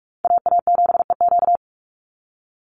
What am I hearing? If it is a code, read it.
Morse code, heard as a keyed tone, about 44 wpm, 700 Hz. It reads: AA7EQ
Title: NA SOTA Chaser Callsigns 45wpm